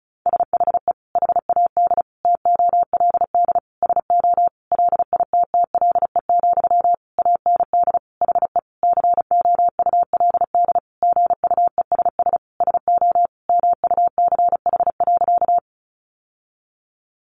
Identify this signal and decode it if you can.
Morse code, heard as a keyed tone, about 34 wpm, 700 Hz. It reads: SHE HAD TOLD SO LITTLE, AND HE COULD GUESS SO MUCH.
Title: Sentences from Top 1500 Words 35wpm